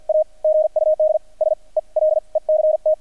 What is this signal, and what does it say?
Morse code, heard as a keyed tone, about 50 wpm, spaced as for 38 wpm, 620 Hz, with radio noise. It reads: A GUD SELECT
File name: archived / CRSnet